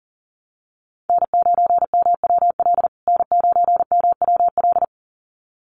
Morse code, heard as a keyed tone, about 40 wpm, 700 Hz. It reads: N9MWL N9MWL
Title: USA Call Signs Repeated 2x 40wpm